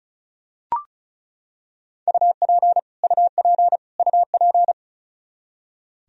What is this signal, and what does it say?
Morse code, heard as a keyed tone, about 34 wpm, 700 Hz. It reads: UP UP UP
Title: Top 100 Words Repeated 35wpm